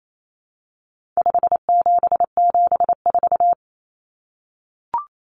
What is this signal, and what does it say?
Morse code, heard as a keyed tone, about 28 wpm, 700 Hz. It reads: 5774
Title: Common Cut Numbers - 4-Digit 28wpm